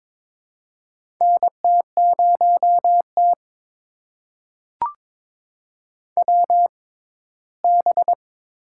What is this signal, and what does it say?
Morse code, heard as a keyed tone, about 21 wpm, spaced as for 15 wpm, 700 Hz. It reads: NT0T WB
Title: USA Call Signs - 4 Characters - 6x Character Spacing - 22wpm